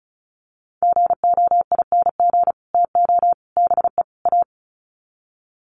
Morse code, heard as a keyed tone, about 34 wpm, 700 Hz. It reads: GOING TO BE A
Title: Sets of 4 Words 35wpm